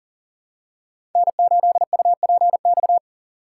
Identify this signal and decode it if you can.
Morse code, heard as a keyed tone, about 40 wpm, 700 Hz. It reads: N8UPX
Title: USA Call Signs - 5 Characters - 2x Character Spacing - 40wpm